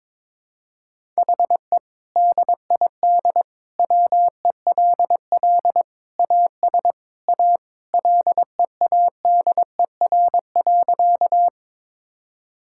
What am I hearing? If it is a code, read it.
Morse code, heard as a keyed tone, about 21 wpm, 700 Hz. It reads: HE DID WELL AS A LEADER.
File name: practice / 22wpm / Sentences with 2-Word Phrases - 22wpm